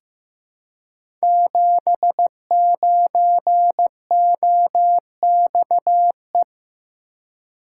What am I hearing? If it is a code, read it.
Morse code, heard as a keyed tone, about 15 wpm, 700 Hz. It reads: 79OXE
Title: Code Groups - 5 Letter:Numbers - 15wpm